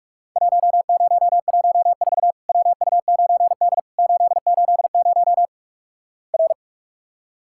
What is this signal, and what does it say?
Morse code, heard as a keyed tone, about 46 wpm, 700 Hz. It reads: 1014 WU9D 880 R
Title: Worked All Europe DX Contest - QTC Exchange - 45wpm